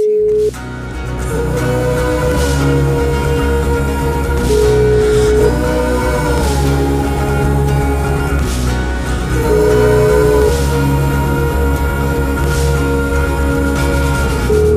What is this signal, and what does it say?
Ringback tone (Europe, 425 Hz), with noise.